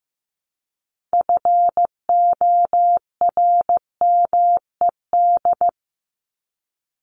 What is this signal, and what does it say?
Morse code, heard as a keyed tone, about 15 wpm, 700 Hz. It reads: FORMED
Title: ED Suffix - Top 100 Words with Base Words 15wpm